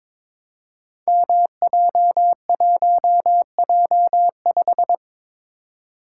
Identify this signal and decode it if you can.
Morse code, heard as a keyed tone, about 23 wpm, 700 Hz. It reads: MJ1J5
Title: Code Groups - 5 Letter:Numbers - 22wpm